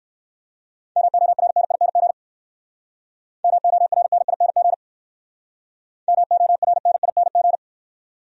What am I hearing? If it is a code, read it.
Morse code, heard as a keyed tone, about 58 wpm, 700 Hz. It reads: MORNING MORNING MORNING